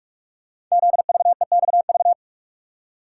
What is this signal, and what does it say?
Morse code, heard as a keyed tone, about 44 wpm, 700 Hz. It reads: ZVEXV
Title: Code Groups - 5 Letters - 45wpm